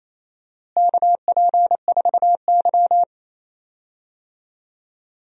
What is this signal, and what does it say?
Morse code, heard as a keyed tone, about 27 wpm, 700 Hz. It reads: KP4Y
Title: DXCC-100 Call Signs - 4 Characters - Repeated 2x - 28wpm